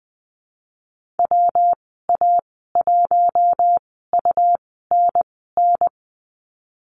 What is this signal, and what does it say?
Morse code, heard as a keyed tone, about 20 wpm, spaced as for 15 wpm, 700 Hz. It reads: WA1UNN
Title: 140 Lesson 23g - Letter f - Call Signs 20wpm